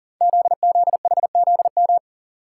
Morse code, heard as a keyed tone, about 39 wpm, 700 Hz. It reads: ZZHZM
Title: Code Groups - 5 Letters - 40wpm